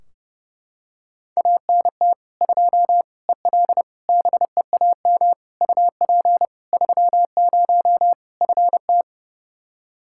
Morse code, heard as a keyed tone, about 32 wpm, 700 Hz. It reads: ANT 2 EL BEAM UP 30 FT
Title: K1USN SST QSO Practice 30wpm